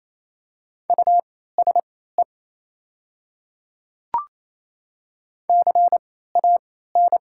Morse code, heard as a keyed tone, about 28 wpm, spaced as for 16 wpm, 690 Hz. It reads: USE CAN
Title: 3x Character Spacing - Top 200 Words - 28wpm